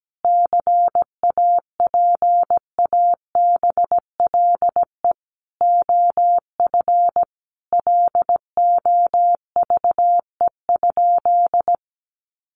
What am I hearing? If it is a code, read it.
Morse code, heard as a keyed tone, about 17 wpm, 700 Hz. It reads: CAPABLE OF LOVE?
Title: Sentences from Top 1200 Words 17wpm